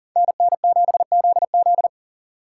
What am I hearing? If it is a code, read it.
Morse code, heard as a keyed tone, about 42 wpm, 700 Hz. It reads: NN7ZZ
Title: Speed Racing - USA 2x2 Call Signs 40wpm